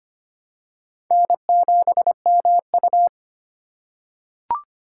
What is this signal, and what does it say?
Morse code, heard as a keyed tone, about 25 wpm, 700 Hz. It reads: N7MU
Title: USA Call Signs - 4 Characters - 4x Character Spacing - 25wpm